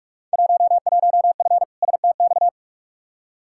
Morse code, heard as a keyed tone, about 44 wpm, 700 Hz. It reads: 11F STX
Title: Field Day Exchanges Encore - 45wpm